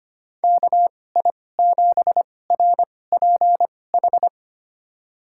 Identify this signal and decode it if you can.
Morse code, heard as a keyed tone, about 25 wpm, spaced as for 18 wpm, 700 Hz. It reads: KI7RPH
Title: USA Call Signs - 6 Characters - 2x Character Spacing - 25wpm